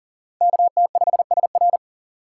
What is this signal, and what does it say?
Morse code, heard as a keyed tone, about 40 wpm, 700 Hz. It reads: KT5SR
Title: USA Call Signs - 5 Characters - 4x Character Spacing - 40wpm